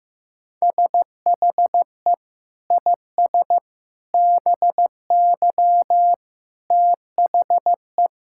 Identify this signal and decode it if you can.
Morse code, heard as a keyed tone, about 15 wpm, 700 Hz. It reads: SHE IS BY THE
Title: Sentences from Top 1600 Words 15wpm